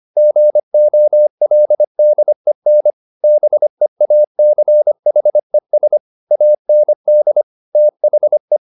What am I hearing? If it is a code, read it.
Morse code, heard as a keyed tone, about 25 wpm, 600 Hz. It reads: GOLDEN BEACHES AND THE
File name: the old man and the sea part 1 25wpm